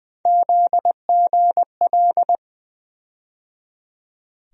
Morse code, heard as a keyed tone, about 20 wpm, 700 Hz. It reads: ZGL
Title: USA Call Sign Postfixes - 3 Letters 20wpm